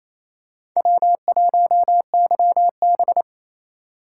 Morse code, heard as a keyed tone, about 28 wpm, 700 Hz. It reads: W1YB
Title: USA Call Signs - 4 Characters - 2x Character Spacing - 28wpm